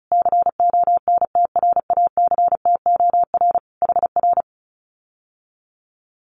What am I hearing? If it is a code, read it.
Morse code, heard as a keyed tone, about 34 wpm, 700 Hz. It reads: CONTRACTOR HR
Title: Common QSO Phrases Encore 35wpm